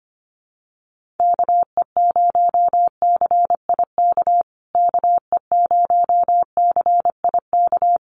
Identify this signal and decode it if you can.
Morse code, heard as a keyed tone, about 25 wpm, 700 Hz. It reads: KE0CIK KE0CIK
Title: USA Call Signs Repeated 3x 25wpm